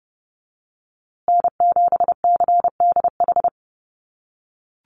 Morse code, heard as a keyed tone, about 30 wpm, 700 Hz. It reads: N7CDH
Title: USA Call Signs Repeated 3x 30wpm